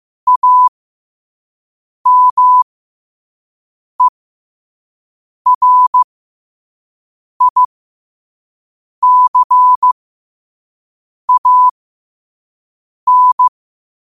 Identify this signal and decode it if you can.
Morse code, heard as a keyed tone, about 15 wpm, spaced as for 5 wpm, 1000 Hz. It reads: AMERICAN